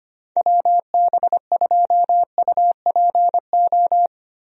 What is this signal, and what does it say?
Morse code, heard as a keyed tone, about 25 wpm, 700 Hz. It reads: WB2UPO
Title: USA Call Signs - 6 Characters - Repeated 3x - 25wpm